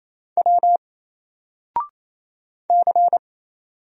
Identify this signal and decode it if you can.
Morse code, heard as a keyed tone, about 27 wpm, 700 Hz. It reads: W C
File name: Single-Letter-Number-28wpm